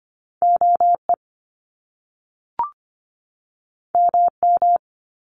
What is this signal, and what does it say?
Morse code, heard as a keyed tone, about 25 wpm, 700 Hz. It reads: OE MM
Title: Top Two Letter Combinations in Words 25wpm